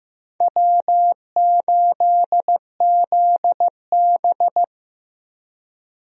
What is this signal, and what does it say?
Morse code, heard as a keyed tone, about 15 wpm, 700 Hz. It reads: W8ZB
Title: USA Call Signs - 4 Characters - Repeated 3x - 15wpm